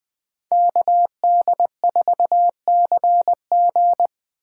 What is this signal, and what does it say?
Morse code, heard as a keyed tone, about 24 wpm, 700 Hz. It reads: KD4CG
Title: Speed Racing - USA 2x2 Call Signs 20wpm